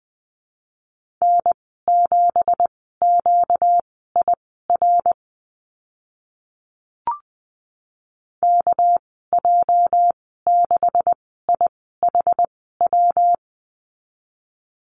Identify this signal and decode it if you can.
Morse code, heard as a keyed tone, about 20 wpm, spaced as for 14 wpm, 700 Hz. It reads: N7QIR KJ6IHW
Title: 238 Lesson 38g - Letter q - Call Signs 20wpm